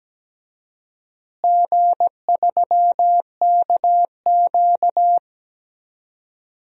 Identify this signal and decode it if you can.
Morse code, heard as a keyed tone, about 17 wpm, 700 Hz. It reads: G3KQ
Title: DXCC-100 Call Signs - 4 Characters - Repeated 2x - 17wpm